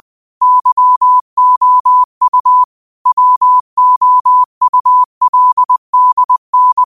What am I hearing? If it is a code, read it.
Morse code, heard as a keyed tone, about 20 wpm, 1000 Hz. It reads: YOU WOULDN
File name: qotdfiles / 20wpm